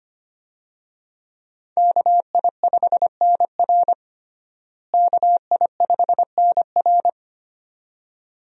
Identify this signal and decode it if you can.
Morse code, heard as a keyed tone, about 25 wpm, 700 Hz. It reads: KI5NR KI5NR
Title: USA Call Signs - 5 Characters - Repeated 2x - 25wpm